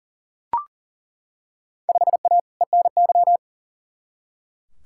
Morse code, heard as a keyed tone, about 40 wpm, 700 Hz. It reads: 5A ENY
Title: Field Day Exchanges Encore - 40wpm